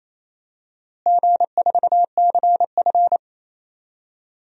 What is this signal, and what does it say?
Morse code, heard as a keyed tone, about 28 wpm, 700 Hz. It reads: G4CF
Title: DXCC-100 Call Signs - 4 Characters - Repeated 3x - 28wpm